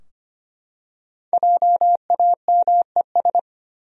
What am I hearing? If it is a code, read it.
Morse code, heard as a keyed tone, about 25 wpm, 700 Hz. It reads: JAMES
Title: USA Common Names - 25wpm